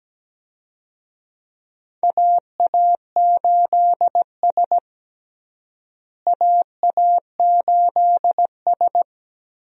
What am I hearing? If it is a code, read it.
Morse code, heard as a keyed tone, about 17 wpm, 700 Hz. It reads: AA8S AA8S
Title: USA Call Signs - 4 Characters - Repeated 3x - 17wpm